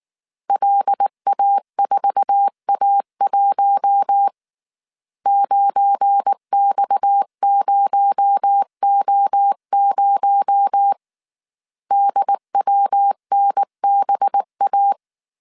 Morse code, heard as a keyed tone, about 19 wpm, 800 Hz. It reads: LA4A1 9X0O0 DWNBA
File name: CW / DET / 1 9 1500C S A 18wpm 2459